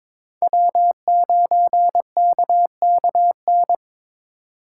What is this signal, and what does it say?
Morse code, heard as a keyed tone, about 22 wpm, 700 Hz. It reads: W9KKN
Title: USA Call Signs - 4x Character Spacing - 22wpm